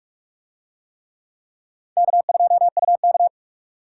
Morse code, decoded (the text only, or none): K2UK